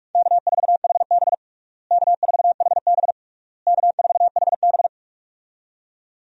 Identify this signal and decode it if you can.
Morse code, heard as a keyed tone, about 46 wpm, 700 Hz. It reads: K4HB K4HB K4HB
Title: USA Call Signs - 4 Characters - Repeated 3x - 45wpm